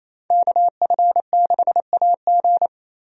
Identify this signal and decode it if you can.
Morse code, heard as a keyed tone, about 28 wpm, 700 Hz. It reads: KF6AG
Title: USA Call Signs - 5 Characters - Repeated 2x - 28wpm